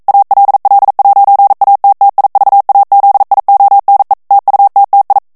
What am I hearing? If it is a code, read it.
Morse code, heard as a keyed tone, about 42 wpm, 800 Hz. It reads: ALL'ATTIVAZIONE TUTTI
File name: morse mp3 / ITALIEN texte clair 1 vitesse 40 mots minute